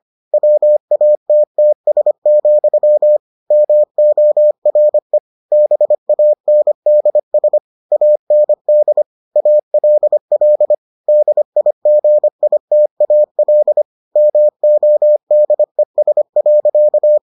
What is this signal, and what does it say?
Morse code, heard as a keyed tone, about 25 wpm, 600 Hz. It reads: WATTS, MORE BANDS AND ALL DIGITAL MODES.